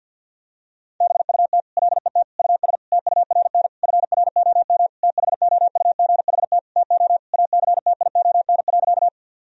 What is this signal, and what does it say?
Morse code, heard as a keyed tone, about 50 wpm, 700 Hz. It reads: BUT LET US TURN FROM THOUGHT TO ACTION.
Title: Sentences from Top 1000 Words - Review Part 4 - 50wpm